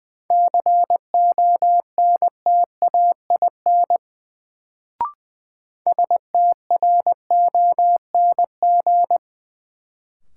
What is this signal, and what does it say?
Morse code, heard as a keyed tone, about 20 wpm, 700 Hz. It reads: CONTAIN STRONG
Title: Sentences from Top 500 Words - New Word Drill - 20wpm